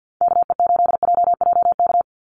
Morse code, heard as a keyed tone, about 48 wpm, 700 Hz. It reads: KE8JJK
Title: USA Call Signs 50wpm